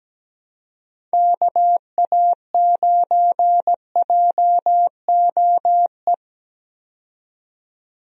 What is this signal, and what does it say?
Morse code, heard as a keyed tone, about 17 wpm, 700 Hz. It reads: KA9JOE
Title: USA Call Signs - 6 Characters - 17wpm